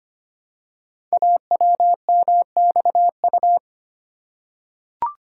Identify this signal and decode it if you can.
Morse code, heard as a keyed tone, about 25 wpm, 700 Hz. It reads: AWMXU
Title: Code Groups - 5 Letters - 25wpm